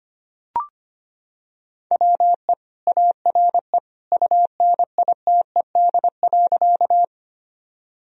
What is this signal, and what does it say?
Morse code, heard as a keyed tone, about 25 wpm, 700 Hz. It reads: WE ARE UNITED.
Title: Sentences from Top 1000 Words - Review Part 4 - 25wpm